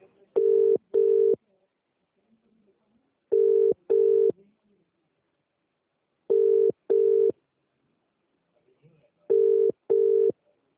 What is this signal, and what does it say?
Telephone call progress tone (UK), heard as a telephone network's tone, about 400, 450 Hz, with speech. Ringback tone.